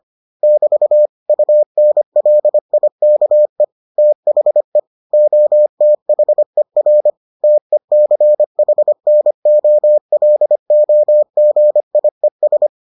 Morse code, heard as a keyed tone, about 25 wpm, 600 Hz. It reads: <BT> UNLIKE THE OTHER TECHNOLOGIES